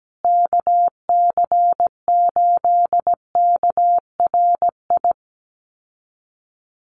Morse code, heard as a keyed tone, about 17 wpm, 700 Hz. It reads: KC8KRI
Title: USA Call Signs Repeated 3x 17wpm